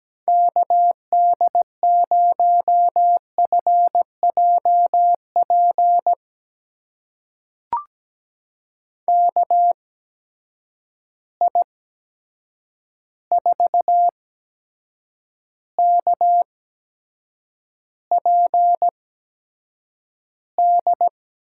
Morse code, heard as a keyed tone, about 17 wpm, spaced as for 7 wpm, 700 Hz. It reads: KD0FJP KI4KPD
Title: USA Call Signs - 6 Characters - 8x Character Spacing - 17wpm